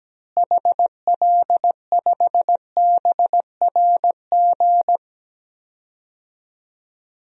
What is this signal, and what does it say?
Morse code, heard as a keyed tone, about 17 wpm, 700 Hz. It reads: HL5BRG
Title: DXCC-100 Call Signs - 6 Characters - Repeated 3x - 17wpm